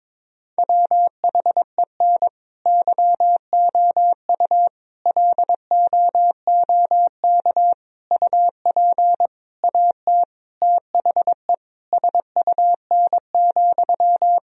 Morse code, heard as a keyed tone, about 22 wpm, 700 Hz. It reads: WHEN YOU LOOK UP AT THE SUN,
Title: Sentences from Top 1000 Words - Review Part 1 - 22wpm